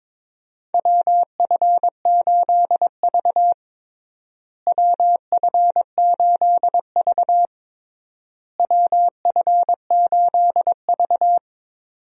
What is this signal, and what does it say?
Morse code, heard as a keyed tone, about 22 wpm, 700 Hz. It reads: WF8V WF8V WF8V
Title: USA Call Signs - 4 Characters - Repeated 3x - 22wpm